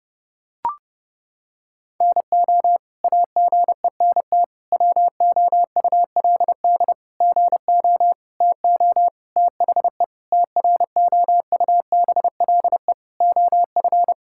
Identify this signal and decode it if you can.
Morse code, heard as a keyed tone, about 30 wpm, 700 Hz. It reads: NO AGENT WOULD GO TO THE TROUBLE OF
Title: Sentences from Top 1600 Words 30wpm